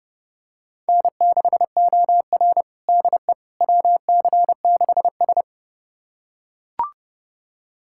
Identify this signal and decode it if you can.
Morse code, heard as a keyed tone, about 30 wpm, 700 Hz. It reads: N6OR DE WC6S
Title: USA Callsign DE Callsign - 4 Characters - 30wpm